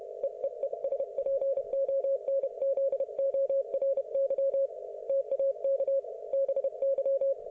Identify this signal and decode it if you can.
Morse code, heard as a keyed tone, about 31 wpm, 530 Hz. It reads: EEE SPONZORY TAK BY